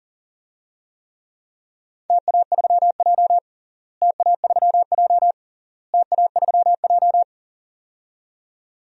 Morse code, heard as a keyed tone, about 37 wpm, 700 Hz. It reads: TA3J TA3J TA3J
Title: DXCC-100 Call Signs - 4 Characters - Repeated 3x - 40wpm